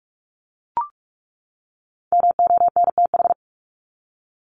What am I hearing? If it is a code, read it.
Morse code, heard as a keyed tone, about 44 wpm, 700 Hz. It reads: MONTH
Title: ES:S Suffix - Top 100 Words with Base Words 45wpm